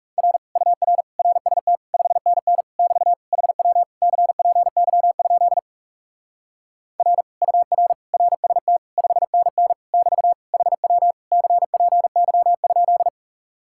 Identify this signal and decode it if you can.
Morse code, heard as a keyed tone, about 40 wpm, 700 Hz. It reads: R UR RST 5NN <BT> HW CPY? R UR RST 5NN <BT> HW CPY?